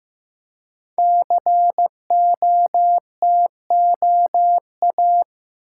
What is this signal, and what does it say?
Morse code, heard as a keyed tone, about 15 wpm, 700 Hz. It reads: COTOA
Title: Code Groups - 5 Letters - 15wpm